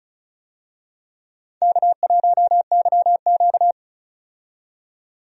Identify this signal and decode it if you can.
Morse code, heard as a keyed tone, about 35 wpm, 700 Hz. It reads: K1YQ
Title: USA Call Signs - 4 Characters - 35wpm